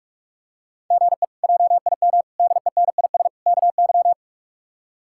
Morse code, heard as a keyed tone, about 46 wpm, 700 Hz. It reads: GE JIM DENIS KY
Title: K1USN SST QSO Practice - 45wpm